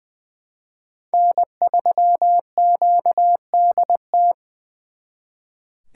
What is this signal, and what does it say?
Morse code, heard as a keyed tone, about 25 wpm, 700 Hz. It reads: N3QDT
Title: Speed Racing - USA 1x3 Call Signs 20wpm